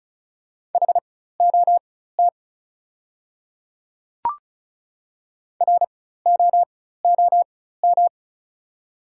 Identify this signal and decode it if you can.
Morse code, heard as a keyed tone, about 36 wpm, spaced as for 16 wpm, 700 Hz. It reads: HOT ROOM
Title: 4x Character Spacing - Top 300 Words - 35wpm